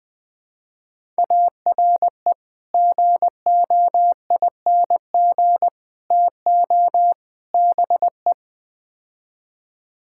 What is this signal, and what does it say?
Morse code, heard as a keyed tone, about 20 wpm, 700 Hz. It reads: ARE GOING TO BE
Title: Build A Phrase - 4-Word Phrases - 20wpm